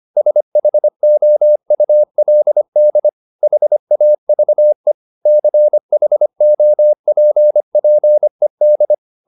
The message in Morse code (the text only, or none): SHOULD HAVE CHOPPED